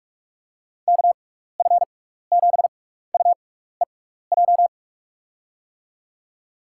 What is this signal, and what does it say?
Morse code, heard as a keyed tone, about 42 wpm, spaced as for 15 wpm, 700 Hz. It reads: KF7UEJ
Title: USA Call Signs - 6 Characters - 6x Character Spacing - 45wpm